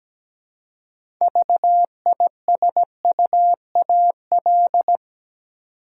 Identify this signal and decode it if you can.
Morse code, heard as a keyed tone, about 17 wpm, 700 Hz. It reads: VISUAL
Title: Six Letter Words - 17wpm